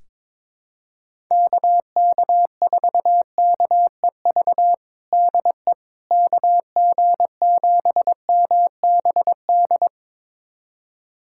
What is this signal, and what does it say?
Morse code, heard as a keyed tone, about 21 wpm, 700 Hz. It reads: KK4KEV DE KG7MBD
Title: USA Callsign DE Callsign - 6 Characters - 22wpm